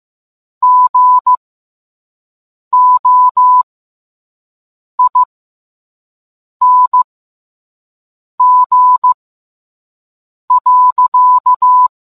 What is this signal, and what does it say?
Morse code, heard as a keyed tone, about 15 wpm, spaced as for 5 wpm, 1000 Hz. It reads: GOING.